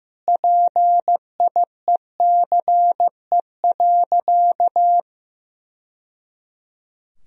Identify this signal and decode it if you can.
Morse code, heard as a keyed tone, about 15 wpm, 700 Hz. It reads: PIECE.
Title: Sentences from Top 1000 Words - Review Part 3 - 15wpm